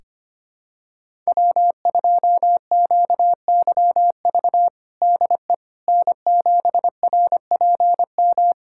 Morse code, heard as a keyed tone, about 25 wpm, 700 Hz. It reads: W2QYV DE N7RPM
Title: USA Callsign DE Callsign - 5 Characters - 25wpm